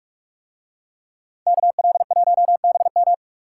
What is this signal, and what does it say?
Morse code, heard as a keyed tone, about 44 wpm, 700 Hz. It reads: KL1BM